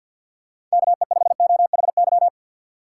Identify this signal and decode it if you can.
Morse code, heard as a keyed tone, about 47 wpm, 700 Hz. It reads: KE5OHY
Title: USA Call Signs - 2x Character Spacing - 50wpm